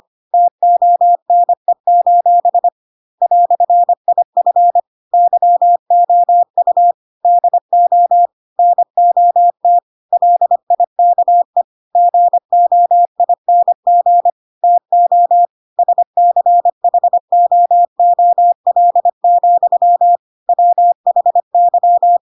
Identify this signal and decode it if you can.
Morse code, heard as a keyed tone, about 25 wpm, 700 Hz. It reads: TONE: "IF YOU DO NOT LIKE GOING TO SCHOOL, WHY
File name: Pinocchio / Pinocchio ch4 25 25 ik1hge